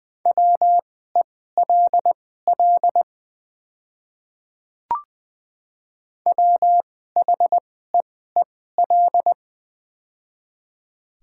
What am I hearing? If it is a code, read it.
Morse code, heard as a keyed tone, about 20 wpm, spaced as for 15 wpm, 700 Hz. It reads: WELL WHEEL
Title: 2x Character Spacing - Top 500 Words - 20wpm